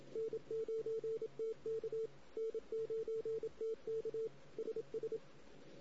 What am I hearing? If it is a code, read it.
Morse code, heard as a keyed tone, about 27 wpm, 450 Hz. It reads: N9TK N9TK SS
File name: CW / 2008-ss-cw-k5tr-op-n5rz -16:00